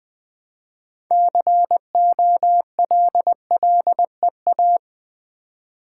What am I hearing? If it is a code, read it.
Morse code, heard as a keyed tone, about 20 wpm, 700 Hz. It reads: COLLEA
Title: Build A Word - 3 Letters At A Time - 9-Letter Words - 20wpm